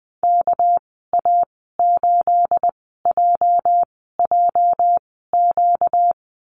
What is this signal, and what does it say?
Morse code, heard as a keyed tone, about 20 wpm, spaced as for 15 wpm, 700 Hz. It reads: KA8JJQ
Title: 238 Lesson 38g - Letter q - Call Signs 20wpm